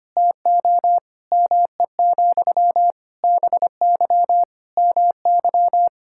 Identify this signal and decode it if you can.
Morse code, heard as a keyed tone, about 25 wpm, 700 Hz. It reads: TO ME, BY MY
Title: Sentences from Top 1000 Words - Review Part 6 - 25wpm